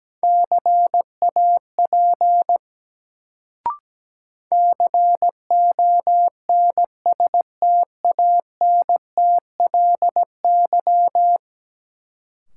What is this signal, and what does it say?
Morse code, heard as a keyed tone, about 17 wpm, 700 Hz. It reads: CAP CONSTANTLY
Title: Sentences from Top 1600 Words - New Word Drill Alt - 17wpm